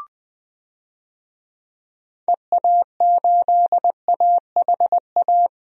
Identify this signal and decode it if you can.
Morse code, heard as a keyed tone, about 20 wpm, 700 Hz. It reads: EA8AHA
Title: DXCC-100 Call Signs - 6 Characters - Repeated 2x - 20wpm